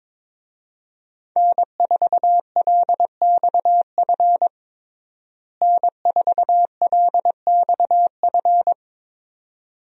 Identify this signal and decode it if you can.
Morse code, heard as a keyed tone, about 22 wpm, 700 Hz. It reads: N4LXF N4LXF
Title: USA Call Signs - 5 Characters - Repeated 2x - 22wpm